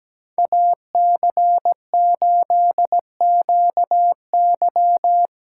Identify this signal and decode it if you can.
Morse code, heard as a keyed tone, about 17 wpm, 700 Hz. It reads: AC8QY
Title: USA Call Signs - 5 Characters - 2x Character Spacing - 17wpm